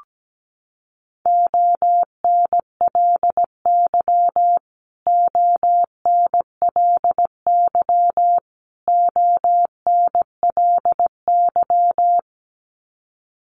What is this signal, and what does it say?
Morse code, heard as a keyed tone, about 17 wpm, 700 Hz. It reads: ONLY ONLY ONLY
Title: Top 200 Words Repeated 17wpm